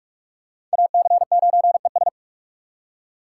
Morse code, heard as a keyed tone, about 44 wpm, 700 Hz. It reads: AC9ES
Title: USA Call Signs - 2x Character Spacing - 45wpm